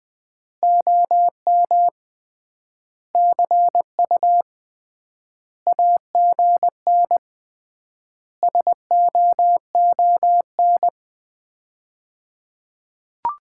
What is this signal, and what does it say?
Morse code, heard as a keyed tone, about 20 wpm, 700 Hz. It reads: OM CU AGN SOON
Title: Common QSO Phrases - 3x Word-Spacing - 20wpm